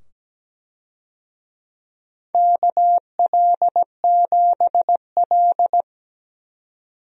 Morse code, heard as a keyed tone, about 17 wpm, 700 Hz. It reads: KL7L
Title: DXCC-100 Call Signs - 4 Characters - 17wpm